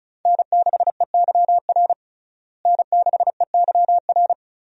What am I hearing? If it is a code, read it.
Morse code, heard as a keyed tone, about 36 wpm, 700 Hz. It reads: N6EYR N6EYR
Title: USA Call Signs - 5 Characters - Repeated 2x - 35wpm